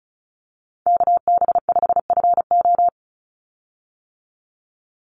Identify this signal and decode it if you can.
Morse code, heard as a keyed tone, about 34 wpm, 700 Hz. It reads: KB5FO
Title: USA Call Signs 35wpm